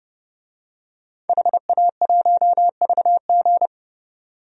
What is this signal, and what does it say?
Morse code, heard as a keyed tone, about 30 wpm, 700 Hz. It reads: HA1VG
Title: DXCC-100 Call Signs - 5 Characters - 30wpm